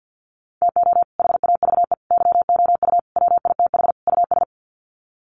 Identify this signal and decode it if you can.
Morse code, heard as a keyed tone, about 50 wpm, 700 Hz. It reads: TO HAVE YOU WITH US
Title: Sets of 5 Words 50wpm